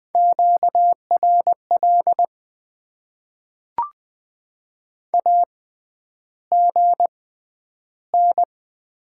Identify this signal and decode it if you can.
Morse code, heard as a keyed tone, about 20 wpm, spaced as for 10 wpm, 700 Hz. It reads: QRL AGN
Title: Top 20 QSO Elements - 6x Character Spacing - 20wpm